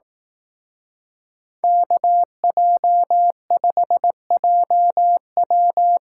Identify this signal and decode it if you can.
Morse code, heard as a keyed tone, about 18 wpm, 700 Hz. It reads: KJ5JW
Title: Speed Racing - USA 2x2 Call Signs 15wpm